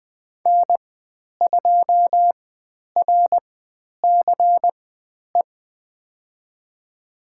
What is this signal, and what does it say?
Morse code, heard as a keyed tone, about 20 wpm, spaced as for 10 wpm, 700 Hz. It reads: N2RCE
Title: 095 Lesson 16g - Letter c - Call Signs 20wpm